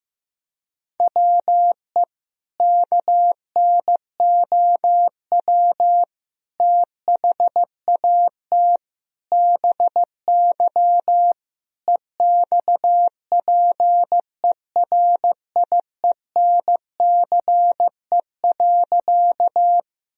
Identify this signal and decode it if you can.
Morse code, heard as a keyed tone, about 15 wpm, 700 Hz. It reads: WE KNOW THAT BY EXPERIENCE.
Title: Sentences from Top 1000 Words - Review Part 3 - 15wpm